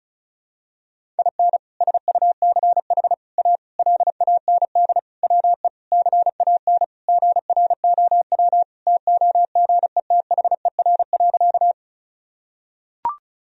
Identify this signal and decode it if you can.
Morse code, heard as a keyed tone, about 36 wpm, 700 Hz. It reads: IN SUCH A LAND WE CAN GROW TOGETHER.
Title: Sentences from Top 1000 Words - Review Part 1 - 35wpm